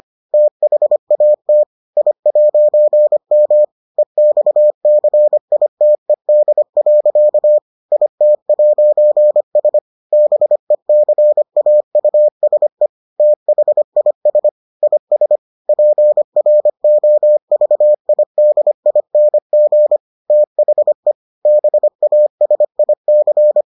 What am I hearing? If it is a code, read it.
Morse code, heard as a keyed tone, about 25 wpm, 600 Hz. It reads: THAT I'M EXCITED. IT'S BECAUSE THIS IS PROVIDING THE BASIC